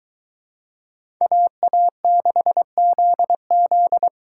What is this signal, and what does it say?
Morse code, heard as a keyed tone, about 23 wpm, 700 Hz. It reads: AA6ZZ